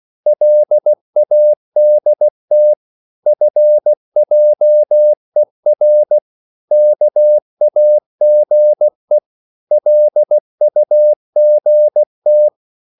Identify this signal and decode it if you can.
Morse code, heard as a keyed tone, about 16 wpm, 600 Hz. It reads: LADT FJER KAGE LUGT
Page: Korte danske ord 16wpm